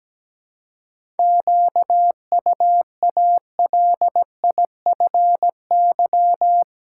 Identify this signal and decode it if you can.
Morse code, heard as a keyed tone, about 17 wpm, 700 Hz. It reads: QUALIFY
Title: Seven Letter Words - 17wpm